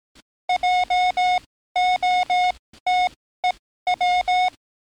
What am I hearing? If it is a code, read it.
Morse code, heard as a keyed tone, about 18 wpm, spaced as for 14 wpm, 730 Hz. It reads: JOTEW
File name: training / NAVPERS 11049to11060 / NAVPERS 11056 RA RB